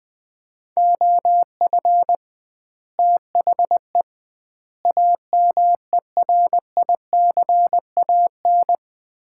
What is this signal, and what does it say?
Morse code, heard as a keyed tone, about 20 wpm, 700 Hz. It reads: OF THE AMERICAN
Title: 2x Word-Spacing - Sets of 3 Words - 20wpm